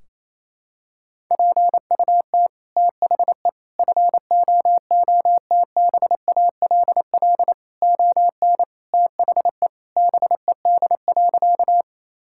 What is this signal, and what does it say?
Morse code, heard as a keyed tone, about 28 wpm, 700 Hz. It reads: PUT THE FOOTBALL ON THE BED.
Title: Sentences from Top 1600 Words 28wpm